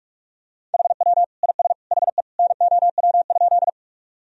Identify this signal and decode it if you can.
Morse code, heard as a keyed tone, about 43 wpm, 700 Hz. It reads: HW IS HE NOW?
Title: Sentences from Top 100 Words and Abbreviations 45wpm